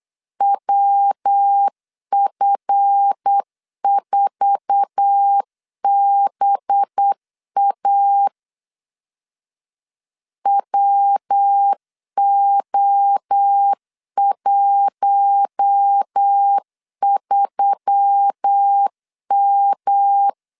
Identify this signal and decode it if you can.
Morse code, heard as a keyed tone, about 8 wpm, 800 Hz. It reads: WF4BA WO13M
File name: LEZIONI / 8 YJ45 / 8 YJ4 8wpm 3246